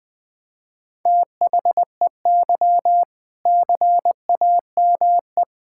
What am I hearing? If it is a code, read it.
Morse code, heard as a keyed tone, about 20 wpm, 700 Hz. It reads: THEY CAME
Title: Sentences from Top 1600 Words 20wpm